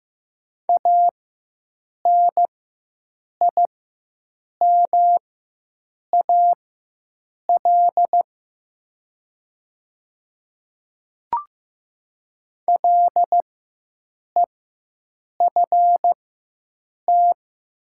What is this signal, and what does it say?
Morse code, heard as a keyed tone, about 15 wpm, spaced as for 7 wpm, 700 Hz. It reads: ANIMAL LEFT
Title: 4x Character Spacing - Top 300 Words - 15wpm